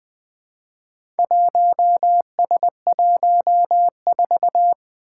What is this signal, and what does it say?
Morse code, heard as a keyed tone, about 20 wpm, 700 Hz. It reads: 1S14
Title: Four Challenging Letter-Number Combinations 20wpm